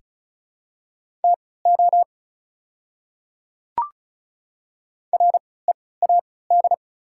Morse code, heard as a keyed tone, about 34 wpm, spaced as for 20 wpm, 700 Hz. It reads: TO READ
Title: 3x Character Spacing - Top 300 Words - 35wpm